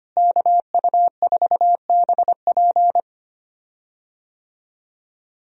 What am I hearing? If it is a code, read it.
Morse code, heard as a keyed tone, about 25 wpm, 700 Hz. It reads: KU4BP
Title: USA Call Signs - 5 Characters - Repeated 2x - 25wpm